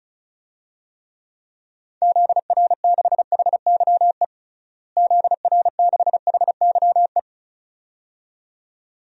Morse code, heard as a keyed tone, about 36 wpm, 700 Hz. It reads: ZR6HYE ZR6HYE
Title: DXCC-100 Call Signs - 6 Characters - Repeated 2x - 35wpm